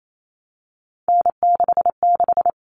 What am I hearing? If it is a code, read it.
Morse code, heard as a keyed tone, about 28 wpm, 700 Hz. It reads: N66
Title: All Cut Numbers - 3-Digit 28wpm